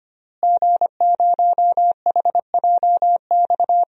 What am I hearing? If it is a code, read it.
Morse code, heard as a keyed tone, about 25 wpm, 700 Hz. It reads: G0HJX
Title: DXCC-100 Call Signs - 5 Characters - Repeated 2x - 25wpm